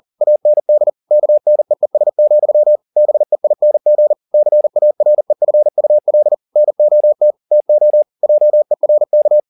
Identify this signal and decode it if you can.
Morse code, heard as a keyed tone, about 40 wpm, 600 Hz. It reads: AND KNEES, BEING CAREFUL NOT TO JERK
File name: the old man and the sea 2 40wpm